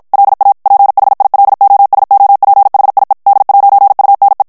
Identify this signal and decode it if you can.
Morse code, heard as a keyed tone, about 56 wpm, 800 Hz. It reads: LA PHILOSOPHIE D'UNE
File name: FRANCAIS TEXTE CLAIR 2 VITESSE 50 MOTS MINUTES